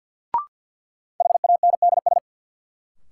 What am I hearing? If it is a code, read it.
Morse code, heard as a keyed tone, about 46 wpm, 700 Hz. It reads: HANDS
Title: Sentences from Top 1000 Words - Review Part 6 - New Word Drill - 50wpm